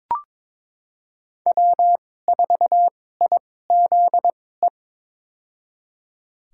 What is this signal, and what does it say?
Morse code, heard as a keyed tone, about 22 wpm, spaced as for 16 wpm, 700 Hz. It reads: W4IZE
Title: USA Call Signs - 5 Characters - 2x Character Spacing - 22wpm